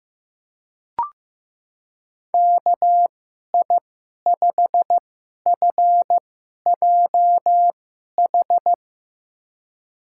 Morse code, heard as a keyed tone, about 15 wpm, spaced as for 11 wpm, 700 Hz. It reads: KI5FJH
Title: USA Call Signs - 6 Characters - 2x Character Spacing - 15wpm